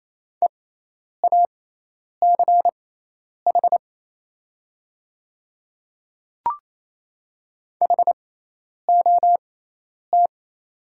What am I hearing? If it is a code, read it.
Morse code, heard as a keyed tone, about 27 wpm, spaced as for 10 wpm, 700 Hz. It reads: EACH HOT
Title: 6x Character Spacing - Top 100 Words - 28wpm